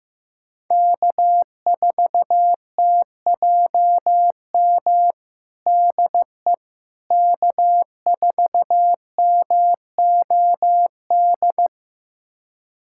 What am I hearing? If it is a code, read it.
Morse code, heard as a keyed tone, about 15 wpm, 700 Hz. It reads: K4TJM DE K4MOD
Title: USA Callsign DE Callsign - 5 Characters - 15wpm